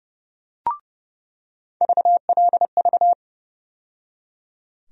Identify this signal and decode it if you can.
Morse code, heard as a keyed tone, about 30 wpm, 700 Hz. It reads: VLV
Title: USA Call Sign Postfixes - 3 Letters 30wpm